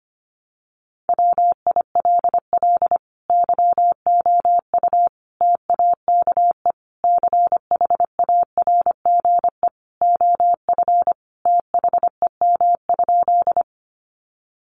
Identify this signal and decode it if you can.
Morse code, heard as a keyed tone, about 25 wpm, 700 Hz. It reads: WILL YOU TAKE CHARGE OF THEM?
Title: Sentences from Top 900 Words 25wpm